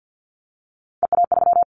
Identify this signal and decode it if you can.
Morse code, heard as a keyed tone, about 50 wpm, 700 Hz. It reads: EA3
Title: Callsign Prefixes for Top 25 DXCC - Repeated 3x - 50wpm